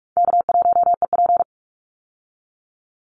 Morse code, heard as a keyed tone, about 44 wpm, 700 Hz. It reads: K1EP
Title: Call Signs Repeated 3x - 45wpm